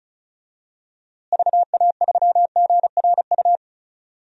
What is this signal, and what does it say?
Morse code, heard as a keyed tone, about 36 wpm, 700 Hz. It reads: VA3GRU
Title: DXCC-100 Call Signs - 6 Characters - Repeated 2x - 35wpm